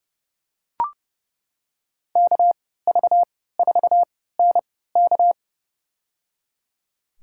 Morse code, heard as a keyed tone, about 30 wpm, spaced as for 17 wpm, 700 Hz. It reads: KV4NK
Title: USA Call Signs - 5 Characters - 3x Character Spacing - 30wpm